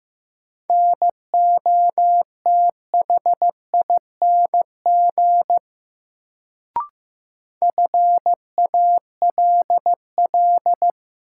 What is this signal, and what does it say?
Morse code, heard as a keyed tone, about 15 wpm, 700 Hz. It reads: NOTHING FALL
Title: Sentences from Top 500 Words - New Word Drill - 15wpm